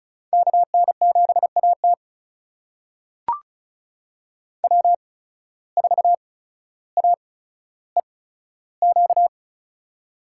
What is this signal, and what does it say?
Morse code, heard as a keyed tone, about 36 wpm, spaced as for 15 wpm, 700 Hz. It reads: KN7AT W4AEQ
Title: USA Call Signs - 5 Characters - 8x Character Spacing - 35wpm